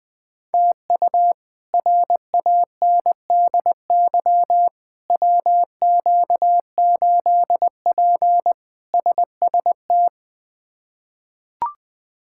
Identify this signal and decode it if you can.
Morse code, heard as a keyed tone, about 20 wpm, 700 Hz. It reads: TU RANDY WQ8P SST
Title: K1USN SST QSO Practice - 20wpm